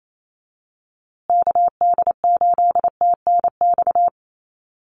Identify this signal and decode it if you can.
Morse code, heard as a keyed tone, about 28 wpm, 700 Hz. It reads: KD8TNX
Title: USA Call Signs Repeated 3x 28wpm